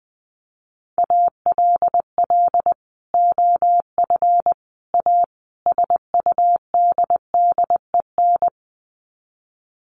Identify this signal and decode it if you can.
Morse code, heard as a keyed tone, about 20 wpm, 700 Hz. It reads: ALL OF A SUDDEN
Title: Sets of 4 Words 20wpm